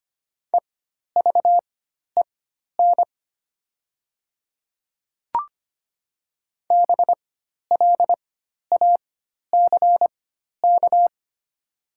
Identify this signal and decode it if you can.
Morse code, heard as a keyed tone, about 25 wpm, spaced as for 12 wpm, 700 Hz. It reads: EVEN BLACK
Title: 4x Character Spacing - Top 500 Words - 25wpm